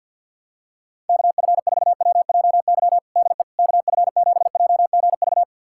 Morse code, heard as a keyed tone, about 51 wpm, 700 Hz. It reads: KF4WJY DE KF7JGV
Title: USA Callsign DE Callsign 50wpm